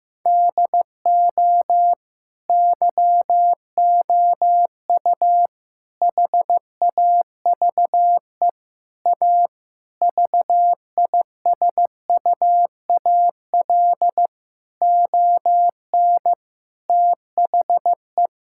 Morse code, heard as a keyed tone, about 15 wpm, 700 Hz. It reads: DO YOU HAVE A VISUAL ON THE
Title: Sentences from Top 1000 Words - Review Part 6 - 15wpm